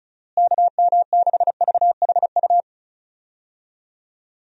Morse code, heard as a keyed tone, about 34 wpm, 700 Hz. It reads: KM6VHU
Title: USA Call Signs - 6 Characters - Repeated 3x - 35wpm